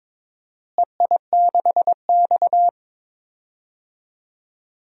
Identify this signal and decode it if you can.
Morse code, heard as a keyed tone, about 21 wpm, 700 Hz. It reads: EI6X
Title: DXCC-100 Call Signs - 4 Characters - Repeated 3x - 22wpm